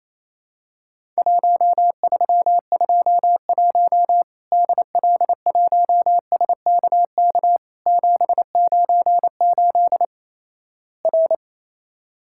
Morse code, heard as a keyed tone, about 28 wpm, 700 Hz. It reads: 1321 DL1SKK 798 R
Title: Worked All Europe DX Contest - QTC Exchange - 28wpm